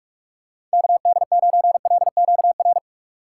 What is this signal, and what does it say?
Morse code, heard as a keyed tone, about 44 wpm, 700 Hz. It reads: KD9LQR